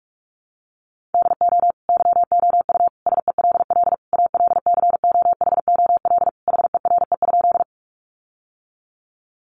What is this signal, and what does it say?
Morse code, heard as a keyed tone, about 46 wpm, 700 Hz. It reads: DO YOU SELL ALCOHOL HERE?
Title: Sentences from Top 1400 Words 45wpm